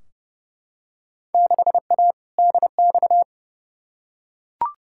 Morse code, heard as a keyed tone, about 30 wpm, 700 Hz. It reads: 6A DX
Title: Field Day Exchanges Encore - 30wpm